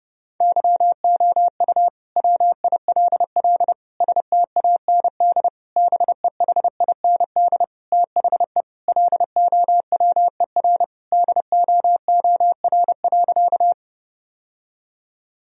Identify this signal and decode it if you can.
Morse code, heard as a keyed tone, about 32 wpm, 700 Hz. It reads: YOU WILL STAND BEHIND THE LOWER DOOR.
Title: Sentences from Top 1000 Words - Review Part 4 - 30wpm